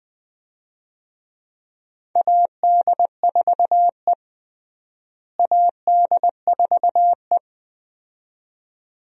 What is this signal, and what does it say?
Morse code, heard as a keyed tone, about 21 wpm, 700 Hz. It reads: AD4E AD4E
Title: USA Call Signs - 4 Characters - Repeated 2x - 20wpm